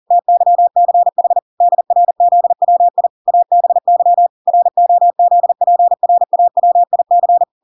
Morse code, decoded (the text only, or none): TYCH DRZWI ABY ROZPRAWIC